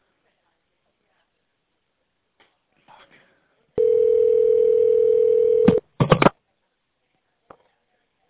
Ringback tone (North America), 440 + 480 Hz.